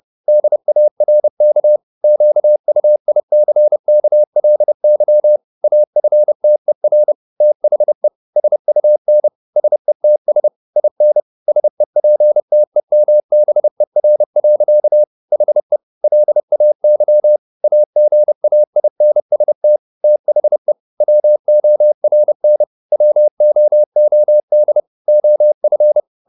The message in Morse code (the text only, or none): DARK QUICKLY AFTER THE SUN SETS IN SEPTEMBER. HE LAY AGAINST THE WORN WOOD OF